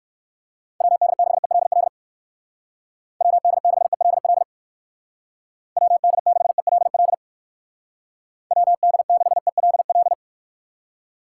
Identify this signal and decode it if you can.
Morse code, heard as a keyed tone, about 56 wpm, 700 Hz. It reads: WD6ELL WD6ELL WD6ELL WD6ELL
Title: Speed Racing - USA 2x3 Call Signs 45wpm